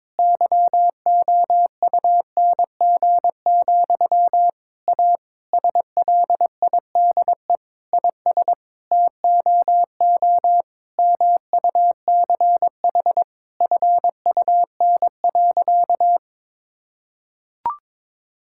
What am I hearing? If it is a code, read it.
Morse code, heard as a keyed tone, about 22 wpm, 700 Hz. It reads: YOUNG, A SLIDE IS TOO MUCH FUN.
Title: Sentences from Top 1600 Words 22wpm